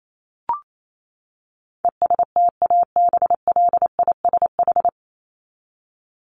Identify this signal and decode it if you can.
Morse code, heard as a keyed tone, about 28 wpm, 700 Hz. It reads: ESTABLISH
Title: ED Suffix - Top 100 Words with Base Words 28wpm